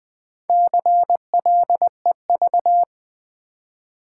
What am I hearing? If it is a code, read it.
Morse code, heard as a keyed tone, about 20 wpm, 690 Hz. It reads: CLEV
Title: Build A Word - 2 Letters At A Time - 6-Letter Words - 20wpm